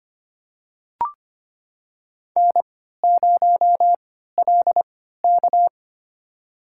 Morse code, heard as a keyed tone, about 25 wpm, spaced as for 14 wpm, 700 Hz. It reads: N0LK
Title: USA Call Signs - 4 Characters - 3x Character Spacing - 25wpm